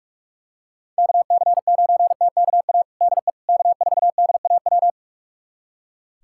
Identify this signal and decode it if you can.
Morse code, heard as a keyed tone, about 43 wpm, 700 Hz. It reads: KC9TKA DE K4DAW
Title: USA Callsign DE Callsign 45wpm